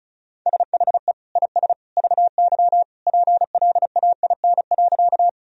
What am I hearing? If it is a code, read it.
Morse code, heard as a keyed tone, about 36 wpm, 700 Hz. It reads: SHE IS VY PLAIN.
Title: Sentences from Top 400 Words and Abbreviations 35wpm